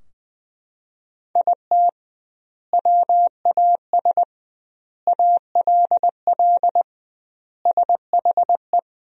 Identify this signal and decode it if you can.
Morse code, heard as a keyed tone, about 20 wpm, 700 Hz. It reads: IT WAS ALL SHE
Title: 2x Word-Spacing - Sentences from Top 400 Words - 20wpm